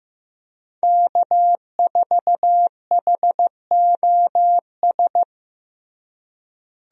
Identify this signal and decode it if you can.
Morse code, heard as a keyed tone, about 15 wpm, 700 Hz. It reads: K4HOS
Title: USA Call Signs - 5 Characters - Repeated 3x - 15wpm